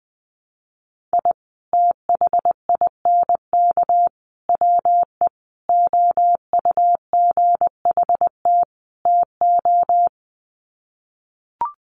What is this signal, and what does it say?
Morse code, heard as a keyed tone, about 20 wpm, 700 Hz. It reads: I THINK WE OUGHT TO
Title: Sets of 5 Words Encore 20wpm